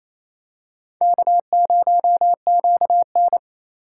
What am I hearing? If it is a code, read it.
Morse code, heard as a keyed tone, about 28 wpm, 700 Hz. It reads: K0QN
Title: USA Call Signs - 2x Character Spacing - 28wpm